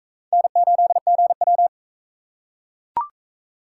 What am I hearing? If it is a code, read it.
Morse code, heard as a keyed tone, about 42 wpm, 700 Hz. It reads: N8GW